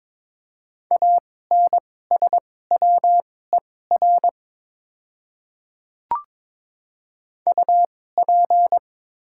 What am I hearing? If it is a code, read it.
Morse code, heard as a keyed tone, about 22 wpm, spaced as for 16 wpm, 690 Hz. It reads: ANSWER UP
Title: 2x Character Spacing - Top 300 Words - 22wpm